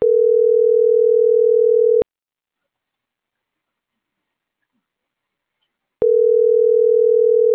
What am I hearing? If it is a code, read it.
Ringback tone (North America), 440 + 480 Hz.
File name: CENTRAL MABE / out celeste.alcantara